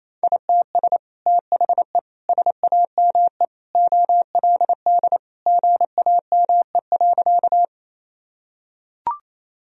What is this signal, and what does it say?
Morse code, heard as a keyed tone, about 28 wpm, 700 Hz. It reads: ITS THE SAME OLD GAME.
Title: Sentences from Top 1000 Words - Review Part 3 - 28wpm